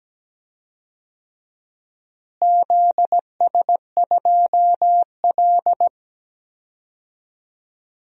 Morse code, heard as a keyed tone, about 17 wpm, 700 Hz. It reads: ZS2L
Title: DXCC-100 Call Signs - 4 Characters - Repeated 3x - 17wpm